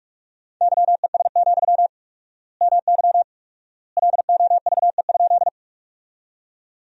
Morse code, heard as a keyed tone, about 46 wpm, 700 Hz. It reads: YES, MY LOVE?
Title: 4x Word-Spacing - Sentences from Top 500 Words - 45wpm